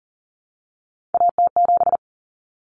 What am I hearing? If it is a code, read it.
Morse code, heard as a keyed tone, about 38 wpm, 700 Hz. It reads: AT7
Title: Common Cut Numbers - 3-Digit 40wpm